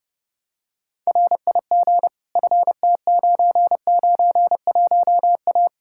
Morse code, heard as a keyed tone, about 30 wpm, 700 Hz. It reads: RIG FT991A
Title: K1USN SST QSO Practice 30wpm